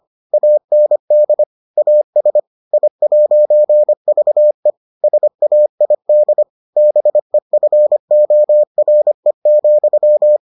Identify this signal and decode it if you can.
Morse code, heard as a keyed tone, about 25 wpm, 600 Hz. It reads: AND AS I'VE SAID BEFORE,